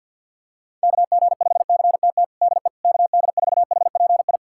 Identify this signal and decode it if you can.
Morse code, heard as a keyed tone, about 49 wpm, 700 Hz. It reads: KG5CTT DE KD4HPI
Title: USA Callsign DE Callsign 50wpm